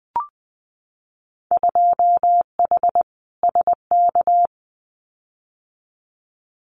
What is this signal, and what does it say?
Morse code, heard as a keyed tone, about 20 wpm, 700 Hz. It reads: 2H SK
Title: Winter Field Day Exchanges Encore 20wpm